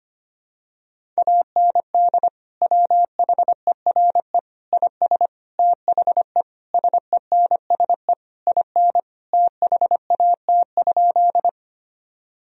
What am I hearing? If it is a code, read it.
Morse code, heard as a keyed tone, about 25 wpm, 700 Hz. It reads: AND WHERE IS THE SENSE IN THAT?
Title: Sentences from Top 1000 Words - Review Part 4 - 25wpm